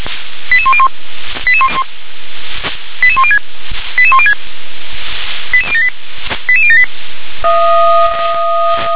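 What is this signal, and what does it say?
Five-tone selective call (ZVEI), heard as quick runs of five tones, tones of about 80 ms. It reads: 88101 88106 88106 88078 88078
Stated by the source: ZVEI five-tone selective call received on an SDR.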